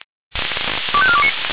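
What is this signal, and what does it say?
Five-tone selective call (ZVEI), heard as quick runs of five tones, tones of about 70 ms. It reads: 25429
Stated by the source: ZVEI five-tone selective call received by SDR.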